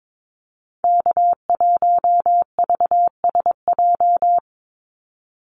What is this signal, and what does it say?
Morse code, heard as a keyed tone, about 22 wpm, 700 Hz. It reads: K1VSJ
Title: Call Signs - 22wpm